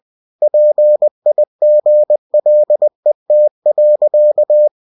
Morse code, heard as a keyed tone, about 20 wpm, 600 Hz. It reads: PIGLET.
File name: cw-books / winnie-the-pooh / winnie-20wpm-page-0072